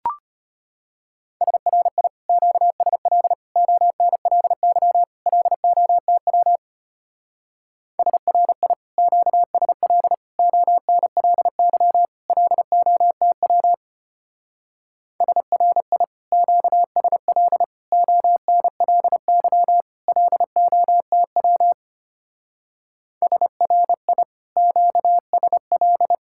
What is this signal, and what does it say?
Morse code, heard as a keyed tone, about 32 wpm, 700 Hz. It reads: SRI QSL ONLY LOTW SRI QSL ONLY LOTW SRI QSL ONLY LOTW SRI QSL